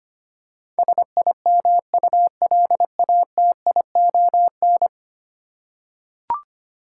Morse code, heard as a keyed tone, about 25 wpm, 700 Hz. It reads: SIMULATION
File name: Ten Letter Words - 25wpm